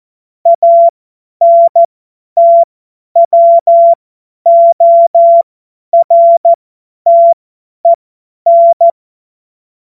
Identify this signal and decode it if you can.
Morse code, heard as a keyed tone, about 14 wpm, spaced as for 10 wpm, 680 Hz. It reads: ANTWORTEN
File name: Morsekurs / Witze / Witz 05 Dezember 2023 50 F70